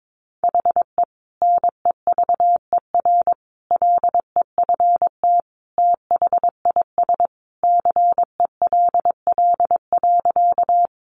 Morse code, heard as a keyed tone, about 22 wpm, 700 Hz. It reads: HE NEVER LEFT THIS CELL.
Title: Sentences from Top 1100 Words 22wpm